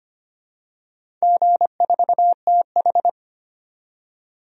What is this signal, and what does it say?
Morse code, heard as a keyed tone, about 25 wpm, 700 Hz. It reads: G4TH
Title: DXCC-100 Call Signs - 4 Characters - 25wpm